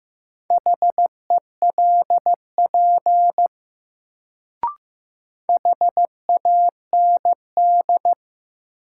Morse code, heard as a keyed tone, about 15 wpm, 700 Hz. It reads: HELP HAND
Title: Sentences from Top 200 Words - New Word Drill - 15wpm